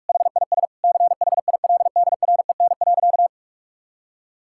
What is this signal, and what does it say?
Morse code, heard as a keyed tone, about 41 wpm, 700 Hz. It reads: HIS CHILDREN.
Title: Sentences from Top 1600 Words 45wpm